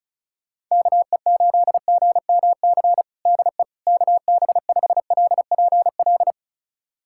Morse code, heard as a keyed tone, about 36 wpm, 700 Hz. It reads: KE8GMC DE KB5LPL
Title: USA Callsign DE Callsign - 6 Characters - 35wpm